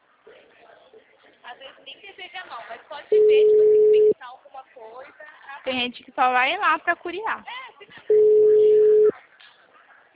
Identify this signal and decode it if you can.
Ringback tone (Europe, 425 Hz), with speech.
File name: out- -2015- - - .44374